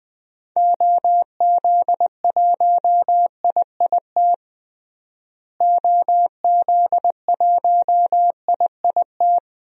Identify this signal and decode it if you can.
Morse code, heard as a keyed tone, about 20 wpm, 700 Hz. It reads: OZ1IIT OZ1IIT
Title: DXCC-100 Call Signs - 6 Characters - Repeated 3x - 20wpm